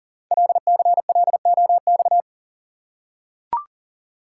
Morse code, heard as a keyed tone, about 42 wpm, 700 Hz. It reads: LCLOX
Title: Code Groups - 5 Letter:Numbers - 40wpm